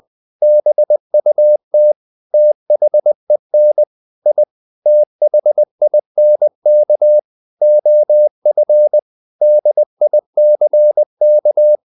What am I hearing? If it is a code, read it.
Morse code, heard as a keyed tone, about 20 wpm, 600 Hz. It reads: BUT THEN I THINK OF DICK
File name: the old man and the sea / the old man and the sea part 1 20wpm